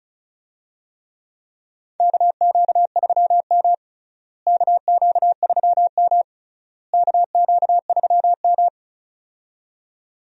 Morse code, heard as a keyed tone, about 36 wpm, 700 Hz. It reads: KQ3M KQ3M KQ3M
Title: USA Call Signs - 4 Characters - Repeated 3x - 35wpm